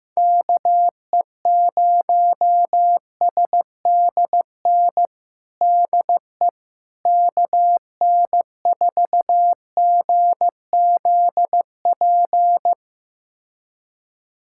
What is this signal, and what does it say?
Morse code, heard as a keyed tone, about 15 wpm, 700 Hz. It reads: KE0SDN DE KN4GZP
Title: USA Callsign DE Callsign - 6 Characters - 15wpm